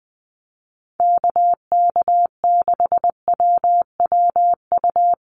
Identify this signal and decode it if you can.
Morse code, heard as a keyed tone, about 20 wpm, 700 Hz. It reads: KK6WWU
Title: USA Call Signs Repeated 3x 20wpm